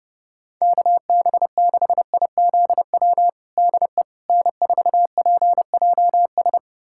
Morse code, heard as a keyed tone, about 32 wpm, 700 Hz. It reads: KB6IZW DE N4PJS
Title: USA Callsign DE Callsign 30wpm